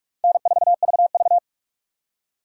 Morse code, heard as a keyed tone, about 47 wpm, 700 Hz. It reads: N4VV
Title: USA Call Signs - 4 Characters - 2x Character Spacing - 45wpm